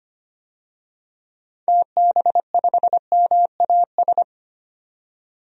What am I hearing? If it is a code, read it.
Morse code, heard as a keyed tone, about 25 wpm, 690 Hz. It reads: TB5MAS
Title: DXCC-100 Call Signs - 6 Characters - 25wpm